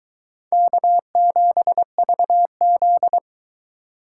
Morse code, heard as a keyed tone, about 22 wpm, 700 Hz. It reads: K7VZ